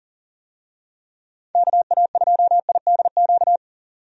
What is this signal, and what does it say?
Morse code, heard as a keyed tone, about 40 wpm, 700 Hz. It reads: KA2IDQ
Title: USA Call Signs - 6 Characters - 40wpm